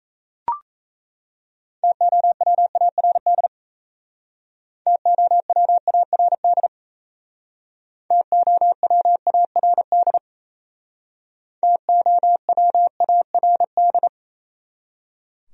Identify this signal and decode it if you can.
Morse code, heard as a keyed tone, about 35 wpm, 700 Hz. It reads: TOWARD TOWARD TOWARD TOWARD
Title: Speed-Racing Top 500 Words 28wpm